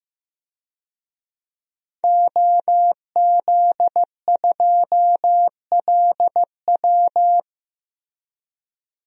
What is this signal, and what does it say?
Morse code, heard as a keyed tone, about 15 wpm, 700 Hz. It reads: OZ2LW
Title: DXCC-100 Call Signs - 5 Characters - Repeated 3x - 15wpm